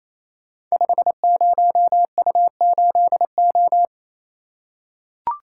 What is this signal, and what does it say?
Morse code, heard as a keyed tone, about 28 wpm, 700 Hz. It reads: 50U8O
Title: Code Groups - 5 Letter:Numbers - 28wpm